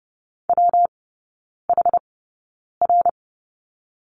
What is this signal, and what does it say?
Morse code, heard as a keyed tone, about 30 wpm, spaced as for 9 wpm, 700 Hz. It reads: WHR
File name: Lesson 19e - Letter w - Three Characters 30wpm